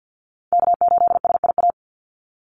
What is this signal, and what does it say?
Morse code, heard as a keyed tone, about 50 wpm, 700 Hz. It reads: K8SIA
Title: Call Signs Repeated 2x - 50wpm